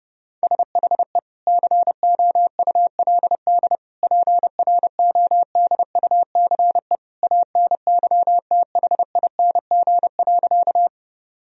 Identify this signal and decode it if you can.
Morse code, heard as a keyed tone, about 30 wpm, 700 Hz. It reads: SHE COULD PRODUCE ANYTHING.
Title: Sentences from Top 1000 Words - Review Part 3 - 30wpm